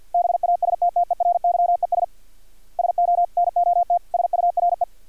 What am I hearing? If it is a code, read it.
Morse code, heard as a keyed tone, about 50 wpm, 700 Hz. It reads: BASTTERYES SO NOT SURE
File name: archived / CRSnet